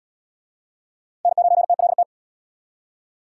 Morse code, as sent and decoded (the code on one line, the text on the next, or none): -. ----. - .-.. -
N9TLT